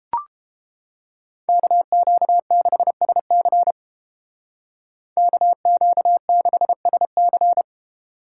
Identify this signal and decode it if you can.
Morse code, heard as a keyed tone, about 29 wpm, 700 Hz. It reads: KQ6SC KQ6SC